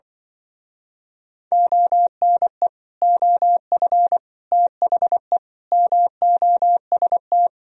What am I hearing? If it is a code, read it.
Morse code, heard as a keyed tone, about 25 wpm, 700 Hz. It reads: ONE OF THE MOST
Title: Speed-Racing - Sets of 5 Words 20wpm